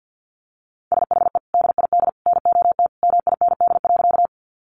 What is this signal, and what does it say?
Morse code, heard as a keyed tone, about 50 wpm, 700 Hz. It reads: SHE DID NOT MIND.
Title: Sentences from Top 600 Words 50wpm